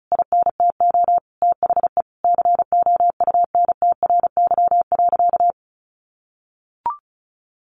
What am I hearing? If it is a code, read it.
Morse code, heard as a keyed tone, about 34 wpm, 700 Hz. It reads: INTO THE COUNTRY.
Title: Sentences from Top 1200 Words 35wpm